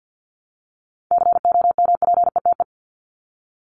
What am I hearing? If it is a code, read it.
Morse code, heard as a keyed tone, about 50 wpm, 700 Hz. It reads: COMPETE
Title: Top 1751-2000 Words 50wpm